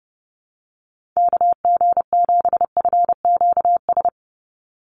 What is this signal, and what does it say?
Morse code, heard as a keyed tone, about 30 wpm, 700 Hz. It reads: KG7FQS
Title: USA Call Signs 30wpm